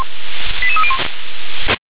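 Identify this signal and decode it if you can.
Five-tone selective call (ZVEI), heard as quick runs of five tones, tones of about 70 ms. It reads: 88301
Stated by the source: ZVEI five-tone selective call received via SDR.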